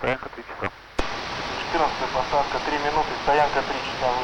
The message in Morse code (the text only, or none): none